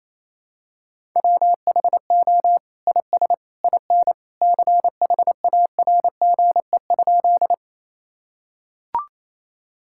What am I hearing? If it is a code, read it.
Morse code, heard as a keyed tone, about 28 wpm, 700 Hz. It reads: WHO IS IN CHARGE?
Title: Sentences from Top 1000 Words - Review Part 6 - 28wpm